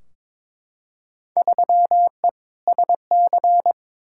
Morse code, heard as a keyed tone, about 22 wpm, 700 Hz. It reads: 3E SC
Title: Field Day Exchanges Encore - 22wpm